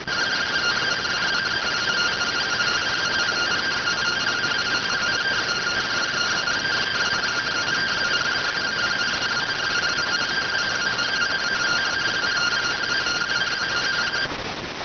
Two-tone teleprinter signal, about 170 Hz shift.